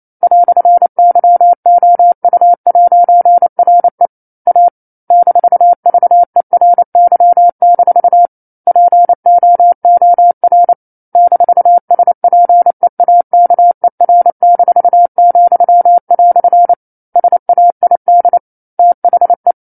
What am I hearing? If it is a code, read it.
Morse code, heard as a keyed tone, about 28 wpm, 700 Hz. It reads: "YOU'RE A -VERY- POOR -SPEAKER-," SAID THE